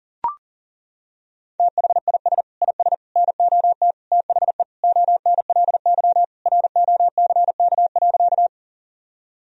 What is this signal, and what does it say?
Morse code, heard as a keyed tone, about 40 wpm, 700 Hz. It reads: THIS IS NOT THE ONLY ROCK.
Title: Sentences from Top 1000 Words - Review Part 2 - 40wpm